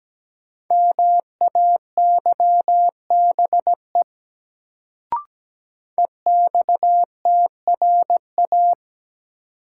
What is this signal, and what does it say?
Morse code, heard as a keyed tone, about 17 wpm, 700 Hz. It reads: MAYBE EXTRA
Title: Sentences from Top 1000 Words - New Word Drill - 17wpm